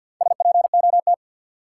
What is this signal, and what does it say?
Morse code, heard as a keyed tone, about 52 wpm, 700 Hz. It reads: SPOT
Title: Sentences from Top 1100 Words - New Word Drill Alt - 50wpm